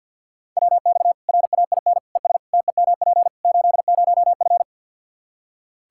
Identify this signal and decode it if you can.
Morse code, heard as a keyed tone, about 46 wpm, 700 Hz. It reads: WX RAIN ES TEMP 80F
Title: K1USN SST QSO Practice 50wpm